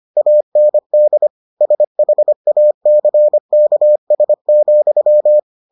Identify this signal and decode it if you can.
Morse code, heard as a keyed tone, about 25 wpm, 600 Hz. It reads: AND SHACKS,